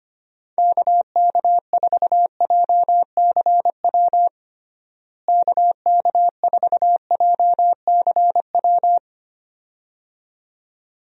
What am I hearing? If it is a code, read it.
Morse code, heard as a keyed tone, about 25 wpm, 700 Hz. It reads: KK4JCW KK4JCW
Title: USA Call Signs - 6 Characters - Repeated 2x - 25wpm